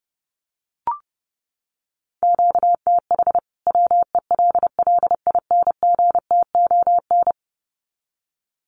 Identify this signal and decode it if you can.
Morse code, heard as a keyed tone, about 30 wpm, 700 Hz. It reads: QTH WELLINGTON
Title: Common QSO Phrases 30wpm